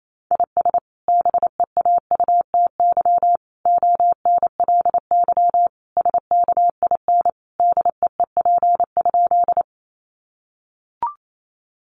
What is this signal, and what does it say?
Morse code, heard as a keyed tone, about 28 wpm, 700 Hz. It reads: IS BEAUTY ONLY SKIN DEEP?
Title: Sentences from Top 1300 Words 28wpm